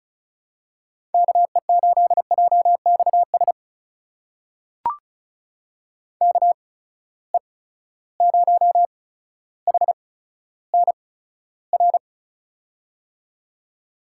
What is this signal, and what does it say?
Morse code, heard as a keyed tone, about 34 wpm, spaced as for 15 wpm, 700 Hz. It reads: KE8JXS KE0HNR
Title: USA Call Signs - 6 Characters - 8x Character Spacing - 35wpm